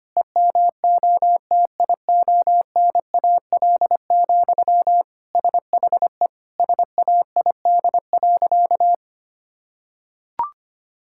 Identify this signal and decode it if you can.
Morse code, heard as a keyed tone, about 25 wpm, 700 Hz. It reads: EMOTIONAL, SHE SAID.
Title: Sentences from Top 1000 Words - Review Part 5 - 25wpm